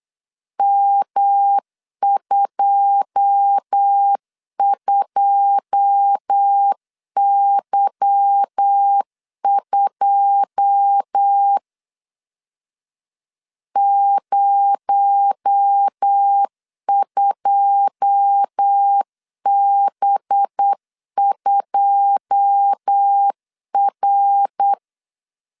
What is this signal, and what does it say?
Morse code, heard as a keyed tone, about 8 wpm, 800 Hz. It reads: M22Y2 02B2R
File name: LEZIONI / 9 2 8wpm 3437